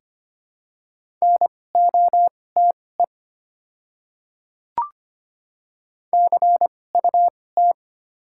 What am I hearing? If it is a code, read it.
Morse code, heard as a keyed tone, about 25 wpm, spaced as for 18 wpm, 690 Hz. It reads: NOTE CUT
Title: 2x Character Spacing - Top 500 Words - 25wpm